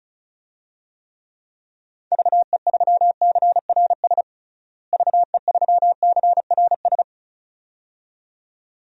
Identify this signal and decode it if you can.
Morse code, heard as a keyed tone, about 36 wpm, 700 Hz. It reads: VE3CRS VE3CRS
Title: DXCC-100 Call Signs - 6 Characters - Repeated 2x - 35wpm